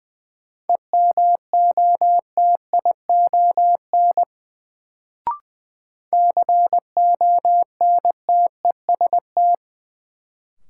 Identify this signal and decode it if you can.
Morse code, heard as a keyed tone, about 20 wpm, 700 Hz. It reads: EMOTION CONTEST
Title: Sentences from Top 1600 Words - New Word Drill Alt - 20wpm